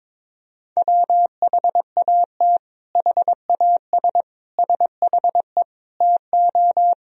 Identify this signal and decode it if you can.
Morse code, heard as a keyed tone, about 21 wpm, 700 Hz. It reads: WHAT HAS SHE TO
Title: Sentences from Top 100 Words and Abbreviations 22wpm